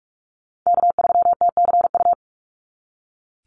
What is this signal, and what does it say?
Morse code, heard as a keyed tone, about 44 wpm, 700 Hz. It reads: K3TCU
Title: NA SOTA Chaser Callsigns 45wpm